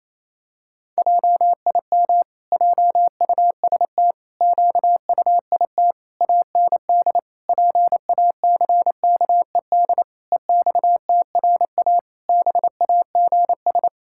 Morse code, heard as a keyed tone, about 27 wpm, 700 Hz. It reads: JIM JUST QUIT AND PACKED EXTRA BAGS
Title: Pangram Challenge v2 - 28wpm